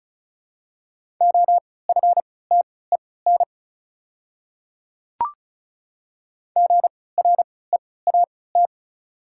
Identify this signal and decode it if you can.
Morse code, heard as a keyed tone, about 34 wpm, spaced as for 20 wpm, 700 Hz. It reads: OFTEN GREAT
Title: 3x Character Spacing - Top 300 Words - 35wpm